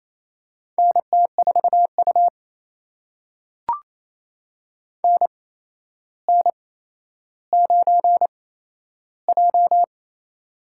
Morse code, heard as a keyed tone, about 28 wpm, spaced as for 12 wpm, 700 Hz. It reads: NT4U NN9J
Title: USA Call Signs - 4 Characters - 8x Character Spacing - 28wpm